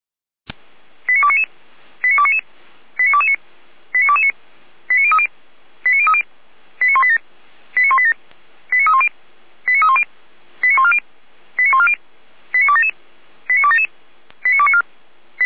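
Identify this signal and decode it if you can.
Five-tone selective call (ZVEI), heard as quick runs of five tones, tones of about 70 ms. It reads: 89299 89299 89229 89229 89030 89030 89187 89187 89310 89310 89140 89140 89277 89277 89294
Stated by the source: ZVEI five-tone selective call 89075 received by SDR.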